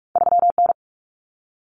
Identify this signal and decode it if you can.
Morse code, heard as a keyed tone, about 46 wpm, 700 Hz. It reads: 3N
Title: Common Cut Numbers - 2-Digit 45wpm